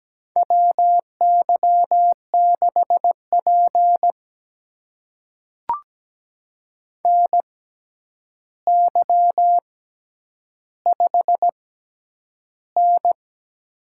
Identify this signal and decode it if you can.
Morse code, heard as a keyed tone, about 17 wpm, spaced as for 9 wpm, 700 Hz. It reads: WY6P NY5N
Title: USA Call Signs - 4 Characters - 6x Character Spacing - 17wpm